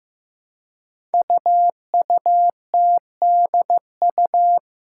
Morse code, heard as a keyed tone, about 15 wpm, 700 Hz. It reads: UUTDU
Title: Code Groups - 5 Letters - 15wpm